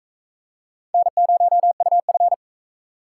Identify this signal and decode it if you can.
Morse code, heard as a keyed tone, about 43 wpm, 700 Hz. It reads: N0UF